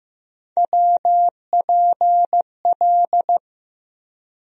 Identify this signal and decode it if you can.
Morse code, heard as a keyed tone, about 15 wpm, 700 Hz. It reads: WPL
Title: USA Call Sign Postfixes - 3 Letters 15wpm